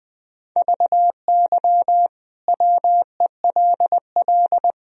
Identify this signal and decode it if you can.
Morse code, heard as a keyed tone, about 20 wpm, 700 Hz. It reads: VY WELL
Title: Sentences from Top 200 Words and Abbreviations 20wpm